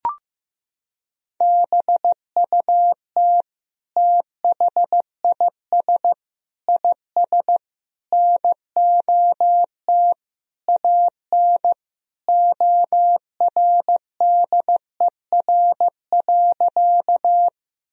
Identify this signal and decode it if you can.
Morse code, heard as a keyed tone, about 15 wpm, 700 Hz. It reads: BUT THIS IS NOT AN ORDER.
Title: Sentences from Top 1000 Words - Review Part 2 - 15wpm